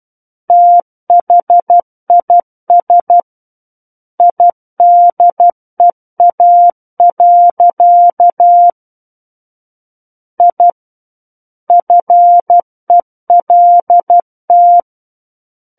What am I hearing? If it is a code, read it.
Morse code, heard as a keyed tone, about 12 wpm, 700 Hz. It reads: THIS IDEA. I FELT
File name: Book One / 03-Chapter 3 - 12 WPM